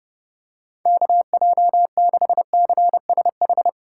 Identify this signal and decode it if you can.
Morse code, heard as a keyed tone, about 31 wpm, 700 Hz. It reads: KJ6CSH